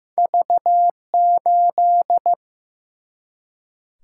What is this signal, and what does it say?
Morse code, heard as a keyed tone, about 15 wpm, 700 Hz. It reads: V8
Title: Single-Letter Single-Number 15wpm